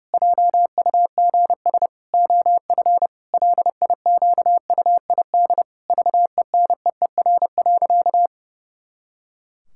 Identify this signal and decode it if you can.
Morse code, heard as a keyed tone, about 30 wpm, 700 Hz. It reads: JUGS OF LIQUID VENEER.
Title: Pangram Challenge v2 - 30wpm